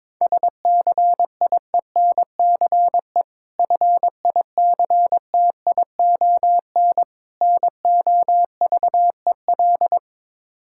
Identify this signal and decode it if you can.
Morse code, heard as a keyed tone, about 22 wpm, 690 Hz. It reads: SCIENCE FICTION NOVEL
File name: Difficult Words with Hints - 22wpm